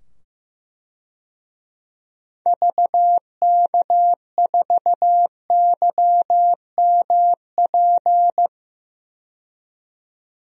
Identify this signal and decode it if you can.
Morse code, heard as a keyed tone, about 16 wpm, 700 Hz. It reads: VK4YMP
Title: DXCC-100 Call Signs - 6 Characters - 15wpm